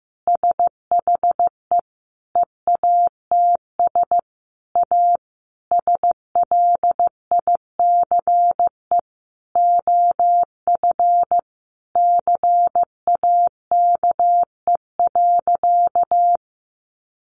Morse code, heard as a keyed tone, about 15 wpm, 700 Hz. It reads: SHE EATS A SLICE OF CAKE.
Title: Easy Phrases 15wpm